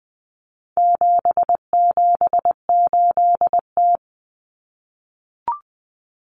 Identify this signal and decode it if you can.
Morse code, heard as a keyed tone, about 20 wpm, 700 Hz. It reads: 778T
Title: Common Cut Numbers - 4-Digit 20wpm